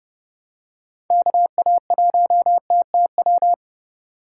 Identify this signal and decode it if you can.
Morse code, heard as a keyed tone, about 30 wpm, 700 Hz. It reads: KA1TTW
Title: USA Call Signs - 6 Characters - 2x Character Spacing - 30wpm